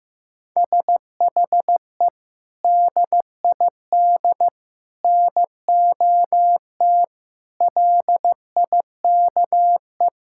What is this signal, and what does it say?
Morse code, heard as a keyed tone, about 15 wpm, 700 Hz. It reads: SHE DID NOT LIKE
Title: Sentences from Top 1000 Words - Review Part 5 - 15wpm